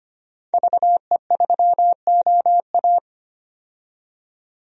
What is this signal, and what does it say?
Morse code, heard as a keyed tone, about 25 wpm, 700 Hz. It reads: VE3OA
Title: DXCC-100 Call Signs - 5 Characters - 25wpm